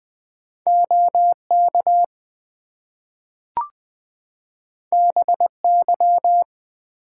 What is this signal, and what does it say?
Morse code, heard as a keyed tone, about 20 wpm, 700 Hz. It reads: OK BY
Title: Two Letter Words - 20wpm